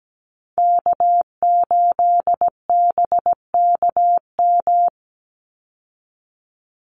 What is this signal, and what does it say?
Morse code, heard as a keyed tone, about 17 wpm, 700 Hz. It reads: K8BKM
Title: Call Signs Repeated 2x - 17wpm